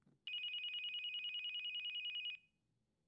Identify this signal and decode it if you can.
There is no code.